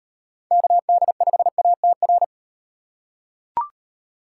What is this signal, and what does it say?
Morse code, heard as a keyed tone, about 37 wpm, 700 Hz. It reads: KD5ATR